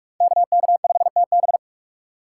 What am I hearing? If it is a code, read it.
Morse code, heard as a keyed tone, about 45 wpm, 700 Hz. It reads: KK5TB